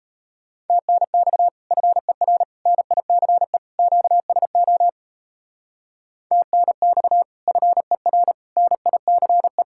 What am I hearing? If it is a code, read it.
Morse code, heard as a keyed tone, about 33 wpm, 700 Hz. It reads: TNX FER NICE QSO TNX FER NICE
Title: Speed-Racing Common QSO Phrases Encore - 28wpm